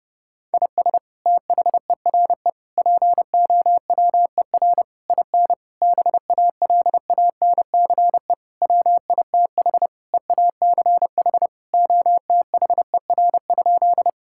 Morse code, heard as a keyed tone, about 30 wpm, 700 Hz. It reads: IS THERE POWER IN BALANCE WITH EACH OTHER?
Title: Sentences from Top 1000 Words - Review Part 6 - 30wpm